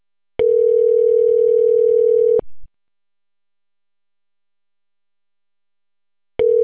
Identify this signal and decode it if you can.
Ringback tone (North America), 440 + 480 Hz.